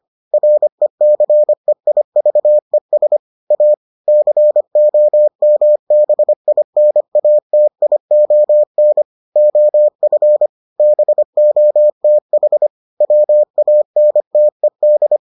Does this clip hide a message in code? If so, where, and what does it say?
Yes: Morse code from 0.3 to 15.2 s, heard as a keyed tone, about 25 wpm, 600 Hz. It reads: RECEIVES A COMBINATION OF BOTH WANTED